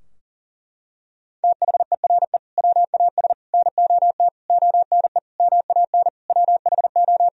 Morse code, heard as a keyed tone, about 40 wpm, 700 Hz. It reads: THERE WAS NOT ONE MAN WHO
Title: Sentences from Top 1000 Words - Review Part 2 - 40wpm